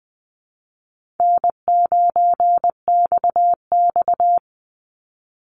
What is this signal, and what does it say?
Morse code, heard as a keyed tone, about 20 wpm, 700 Hz. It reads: N9XX
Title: Call Signs Repeated 2x - 20wpm